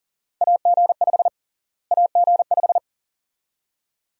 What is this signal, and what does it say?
Morse code, heard as a keyed tone, about 40 wpm, 700 Hz. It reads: AG5 AG5
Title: USA Call Sign Prefixes - Repeated 2x - 40wpm